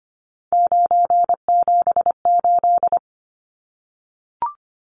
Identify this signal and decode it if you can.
Morse code, heard as a keyed tone, about 25 wpm, 700 Hz. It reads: 978
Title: Numbers - 3-Digit 25wpm